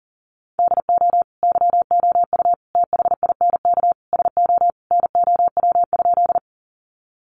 Morse code, heard as a keyed tone, about 40 wpm, 700 Hz. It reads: DO YOU THINK SO NOW?
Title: Sentences from Top 200 Words 40wpm